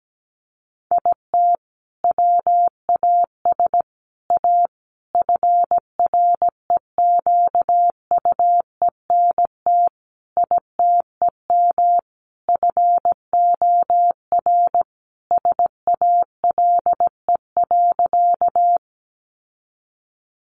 Morse code, heard as a keyed tone, about 17 wpm, 700 Hz. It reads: IT WAS A FREQUENT ITEM FOR SALE.
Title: Sentences from Top 1400 Words 17wpm